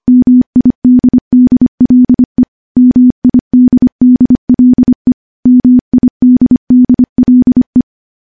Morse code, heard as a keyed tone, about 25 wpm, 250 Hz. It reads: MIDDLE MIDDLE MIDDLE
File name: morse / morse-training-25WPM